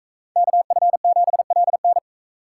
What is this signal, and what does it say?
Morse code, heard as a keyed tone, about 40 wpm, 700 Hz. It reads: KF7LN